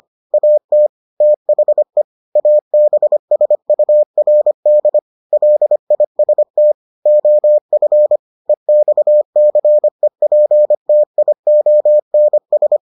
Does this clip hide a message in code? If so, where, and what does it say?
Yes: Morse code from 0.3 to 12.8 s, heard as a keyed tone, about 25 wpm, 600 Hz. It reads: AT THE ABSURD LIST OF EXCEPTIONS